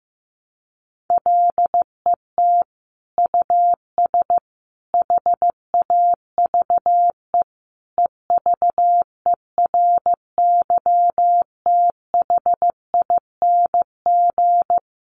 Morse code, heard as a keyed tone, about 15 wpm, 700 Hz. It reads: LET US HAVE EVERYTHING
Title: Sentences from Top 900 Words 15wpm